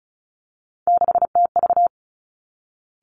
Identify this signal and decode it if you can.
Morse code, heard as a keyed tone, about 36 wpm, 700 Hz. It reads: 6TV
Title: Three Challenging Letter-Number Combinations 35wpm